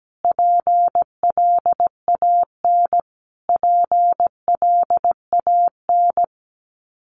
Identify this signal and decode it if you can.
Morse code, heard as a keyed tone, about 17 wpm, 700 Hz. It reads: PLAN PLAN
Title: Top 500 Words Repeated 17wpm